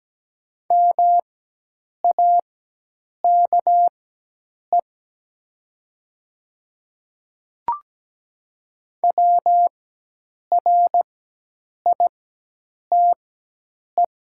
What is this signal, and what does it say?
Morse code, heard as a keyed tone, about 17 wpm, spaced as for 8 wpm, 700 Hz. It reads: MAKE WRITE
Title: 4x Character Spacing - Top 100 Words - 17wpm